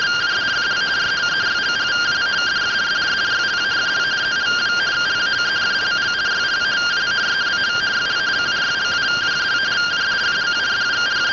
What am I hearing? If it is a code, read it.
Two-tone teleprinter signal, about 170 Hz shift.
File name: findings / 81 axial iodine 262.gen-07 rtty rx